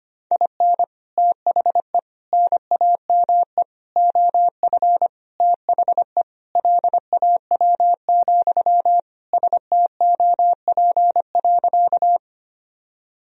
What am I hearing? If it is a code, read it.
Morse code, heard as a keyed tone, about 25 wpm, 700 Hz. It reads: IN THE NAME OF THE LAW, STOP.
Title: Sentences from Top 1000 Words - Review Part 3 - 25wpm